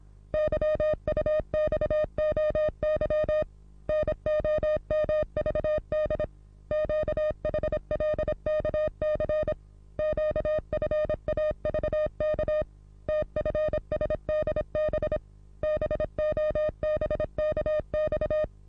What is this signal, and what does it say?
Morse code, heard as a keyed tone, about 26 wpm, 610 Hz. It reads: YUXOY NOMVD QHLKC QFAVK TFSDB BOBKX
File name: repos / Morsekurs / DJ4UF Morsen27